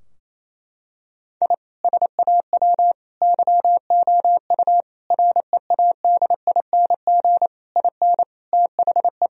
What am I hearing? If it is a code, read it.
Morse code, heard as a keyed tone, about 28 wpm, 700 Hz. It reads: I SAW YOU READING IN THE
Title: Sentences from Top 1000 Words - Review Part 4 - 28wpm